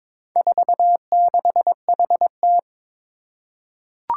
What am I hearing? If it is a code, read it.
Morse code, heard as a keyed tone, about 22 wpm, 700 Hz. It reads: 46HT
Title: Four Challenging Letter-Number Combinations 22wpm